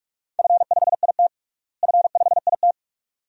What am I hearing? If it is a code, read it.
Morse code, heard as a keyed tone, about 46 wpm, 700 Hz. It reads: F5IT F5IT
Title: DXCC-100 Call Signs - 4 Characters - Repeated 2x - 45wpm